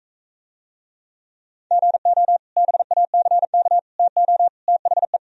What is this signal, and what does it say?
Morse code, heard as a keyed tone, about 43 wpm, 700 Hz. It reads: GO BACK TO THE